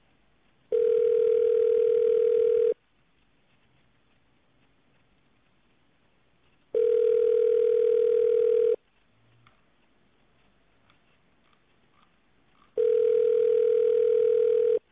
Ringback tone (North America), 440 + 480 Hz.